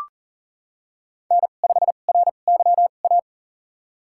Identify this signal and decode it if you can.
Morse code, heard as a keyed tone, about 40 wpm, spaced as for 29 wpm, 700 Hz. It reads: N5RYA
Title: USA Call Signs - 5 Characters - 2x Character Spacing - 40wpm